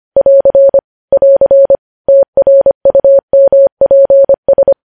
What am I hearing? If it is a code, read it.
Morse code, heard as a keyed tone, about 25 wpm, 550 Hz. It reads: <AR> <AR> TRUMPS
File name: podcast / News headlines-1x-25x25